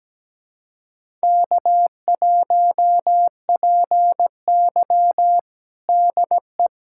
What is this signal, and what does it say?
Morse code, heard as a keyed tone, about 17 wpm, 700 Hz. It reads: K1PY DE
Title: USA Callsign DE Callsign - 4 Characters - 17wpm